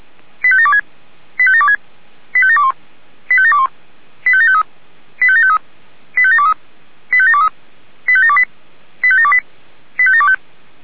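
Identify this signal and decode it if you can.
Five-tone selective call (ZVEI), heard as quick runs of five tones, tones of about 70 ms. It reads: 85726 85726 85721 85721 85753 85753 85723 85723 85728 85728 85725
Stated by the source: ZVEI five-tone selective call 85759, received on SDR.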